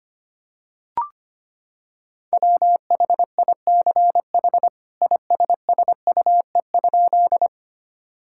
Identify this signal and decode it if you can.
Morse code, heard as a keyed tone, about 25 wpm, 700 Hz. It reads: WHICH ISSUE?
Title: Sentences from Top 1000 Words - Review Part 6 - 25wpm